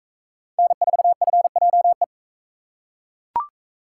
Morse code, heard as a keyed tone, about 38 wpm, 700 Hz. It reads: N4FJE